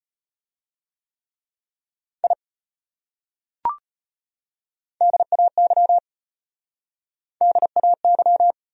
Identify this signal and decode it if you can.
Morse code, heard as a keyed tone, about 30 wpm, 700 Hz. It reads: I DAY DAY
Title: Speed-Racing Top 200 Words 25wpm